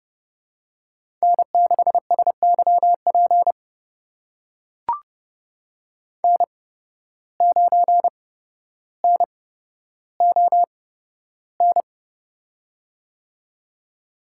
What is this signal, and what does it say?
Morse code, heard as a keyed tone, about 30 wpm, spaced as for 13 wpm, 700 Hz. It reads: N6SYP N9NON
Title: USA Call Signs - 5 Characters - 8x Character Spacing - 30wpm